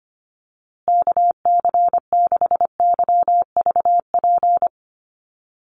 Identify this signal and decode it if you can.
Morse code, heard as a keyed tone, about 25 wpm, 700 Hz. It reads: KC6YVP
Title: USA Call Signs Repeated 3x 25wpm